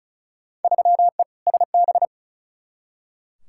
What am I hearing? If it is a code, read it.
Morse code, heard as a keyed tone, about 34 wpm, 700 Hz. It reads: 3E SB
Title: Field Day Exchanges Encore - 35wpm